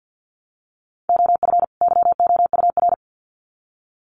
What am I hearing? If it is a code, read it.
Morse code, heard as a keyed tone, about 50 wpm, 700 Hz. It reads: OF YOUR
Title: Sets of 2 Words 50wpm